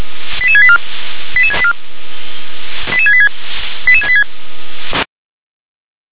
Five-tone selective call (ZVEI), heard as quick runs of five tones, tones of about 70 ms. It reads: 88684 88684 88686 88686